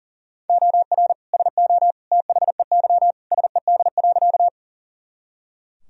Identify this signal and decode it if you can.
Morse code, heard as a keyed tone, about 40 wpm, 700 Hz. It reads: OR SO THEY SED.
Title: Sentences from Top 100 Words and Abbreviations 40wpm